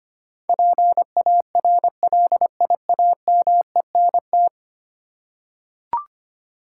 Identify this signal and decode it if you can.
Morse code, heard as a keyed tone, about 25 wpm, 700 Hz. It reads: PARLIAMENT
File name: Ten Letter Words - 25wpm